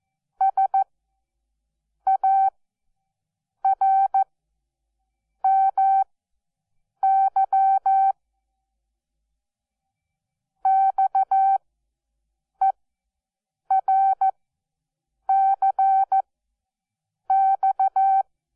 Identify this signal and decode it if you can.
Morse code, heard as a keyed tone, about 14 wpm, spaced as for 6 wpm, 780 Hz. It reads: SARMY XERCX
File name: Onlinelehrgang / morsen / DJ4UF Morsen12